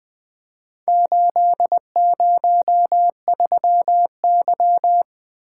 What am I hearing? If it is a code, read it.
Morse code, heard as a keyed tone, about 22 wpm, 700 Hz. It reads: 803Y
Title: Speed-Racing - Four Letter-Number Combinations - 20wpm